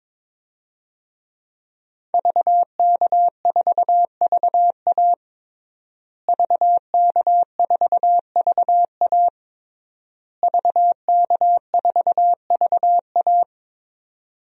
Morse code, heard as a keyed tone, about 23 wpm, 700 Hz. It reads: VK4VA VK4VA VK4VA
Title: DXCC-100 Call Signs - 5 Characters - Repeated 3x - 22wpm